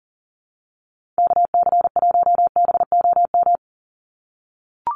Morse code, heard as a keyed tone, about 40 wpm, 700 Hz. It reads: KC1BOM
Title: USA Call Signs 40wpm